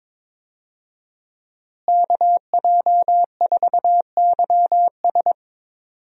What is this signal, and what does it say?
Morse code, heard as a keyed tone, about 22 wpm, 700 Hz. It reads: KJ4YS
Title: USA Call Signs - 5 Characters - Repeated 3x - 22wpm